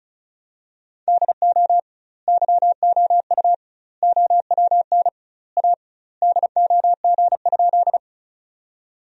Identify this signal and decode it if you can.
Morse code, heard as a keyed tone, about 36 wpm, 700 Hz. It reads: DO YOU OWN A DOG?
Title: 2x Word-Spacing - Sentences from Top 400 Words - 35wpm